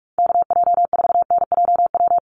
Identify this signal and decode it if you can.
Morse code, heard as a keyed tone, about 44 wpm, 700 Hz. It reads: KJ4NJW
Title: USA Call Signs 45wpm